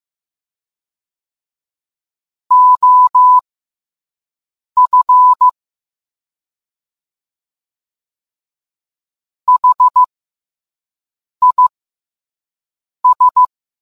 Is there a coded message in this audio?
Yes: Morse code from 2.5 to 13.4 s, heard as a keyed tone, about 15 wpm, spaced as for 5 wpm, 1000 Hz. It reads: OF HIS